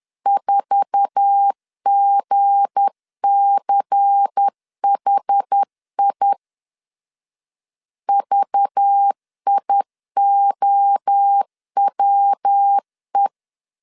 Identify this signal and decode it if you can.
Morse code, heard as a keyed tone, about 10 wpm, 800 Hz. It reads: 4GCHI VIOWE
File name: LEZIONI / 10 SIGNS / 10A SLASH 10wpm 2808